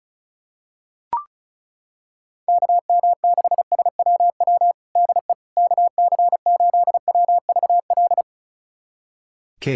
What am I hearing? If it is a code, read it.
Morse code, heard as a keyed tone, about 35 wpm, 700 Hz. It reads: KM6SWW DE KC8WVL
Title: USA Callsign DE Callsign 35wpm